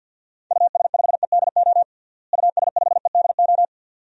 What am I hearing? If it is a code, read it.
Morse code, heard as a keyed tone, about 47 wpm, 700 Hz. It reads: US5EDO US5EDO
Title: DXCC-100 Call Signs - 6 Characters - Repeated 2x - 50wpm